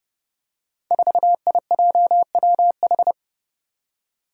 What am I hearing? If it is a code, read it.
Morse code, heard as a keyed tone, about 29 wpm, 700 Hz. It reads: 4IJWH
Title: Five Challenging Letter-Number Combinations 30wpm